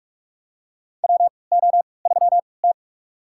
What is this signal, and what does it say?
Morse code, heard as a keyed tone, about 44 wpm, spaced as for 25 wpm, 690 Hz. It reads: WO3T
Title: USA Call Signs - 4 Characters - 3x Character Spacing - 45wpm